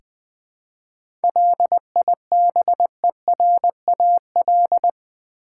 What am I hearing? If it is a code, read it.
Morse code, heard as a keyed tone, about 20 wpm, 700 Hz. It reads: LIBERAL
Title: 7 Letter Words - First 3 Letters Given - 20wpm